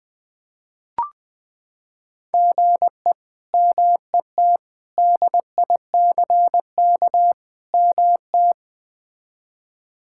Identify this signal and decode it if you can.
Morse code, heard as a keyed tone, about 20 wpm, 700 Hz. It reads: GE MET DICK MT
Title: K1USN SST QSO Practice - 20wpm